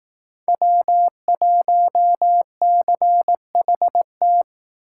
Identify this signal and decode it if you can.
Morse code, heard as a keyed tone, about 19 wpm, 700 Hz. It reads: W1CHT